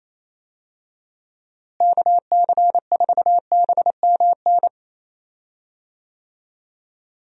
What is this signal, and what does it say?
Morse code, heard as a keyed tone, about 28 wpm, 700 Hz. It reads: KC4BMN
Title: USA Call Signs - 6 Characters - Repeated 3x - 28wpm